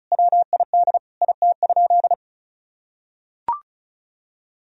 Morse code, heard as a keyed tone, about 34 wpm, 700 Hz. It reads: WID IT?
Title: Sentences from Top 100 Words and Abbreviations 35wpm